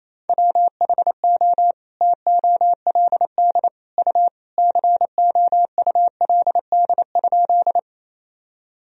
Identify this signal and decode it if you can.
Morse code, heard as a keyed tone, about 28 wpm, 700 Hz. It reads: WHO TOLD U COULD?
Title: Sentences from Top 400 Words and Abbreviations 28wpm